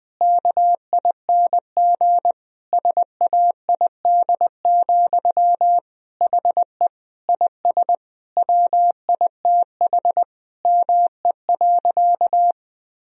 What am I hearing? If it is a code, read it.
Morse code, heard as a keyed tone, about 20 wpm, 700 Hz. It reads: KING SAID, HE IS WITH ME.
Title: Sentences from Top 1000 Words - Review Part 3 - 20wpm